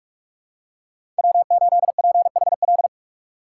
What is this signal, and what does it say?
Morse code, heard as a keyed tone, about 46 wpm, 700 Hz. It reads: W8PHL
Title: USA Call Signs - 5 Characters - 4x Character Spacing - 45wpm